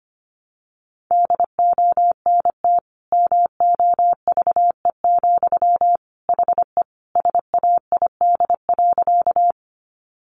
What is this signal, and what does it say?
Morse code, heard as a keyed tone, about 25 wpm, 700 Hz. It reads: DONT MOVE, HE SAID.
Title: Sentences from Top 300 Words 25wpm